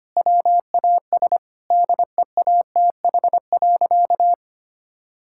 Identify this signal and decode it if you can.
Morse code, heard as a keyed tone, about 25 wpm, 700 Hz. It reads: WAS DEATH.
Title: Sentences from Top 1600 Words 25wpm